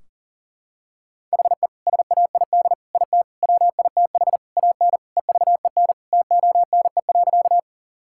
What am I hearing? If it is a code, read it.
Morse code, heard as a keyed tone, about 40 wpm, 700 Hz. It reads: HE SAID IT WITH AN EVEN TONE.
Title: Sentences from Top 1600 Words 40wpm